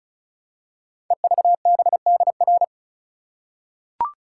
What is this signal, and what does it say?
Morse code, heard as a keyed tone, about 36 wpm, 700 Hz. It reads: EVBDR
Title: Code Groups - 5 Letter:Numbers - 35wpm